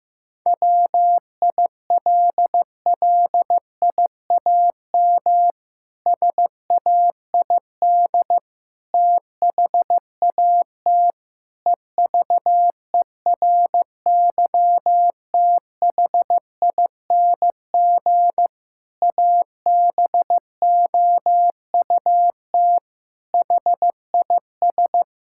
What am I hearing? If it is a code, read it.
Morse code, heard as a keyed tone, about 16 wpm, 700 Hz. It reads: WILLIAM SAID THAT EVERYTHING ABOUT HIS
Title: Pangram Challenge v2 - 15wpm